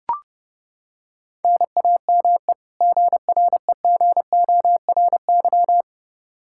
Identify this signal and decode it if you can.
Morse code, heard as a keyed tone, about 30 wpm, 700 Hz. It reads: NAME GREGORY
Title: Practice QSO - Exchange 1 - Beginning - 30wpm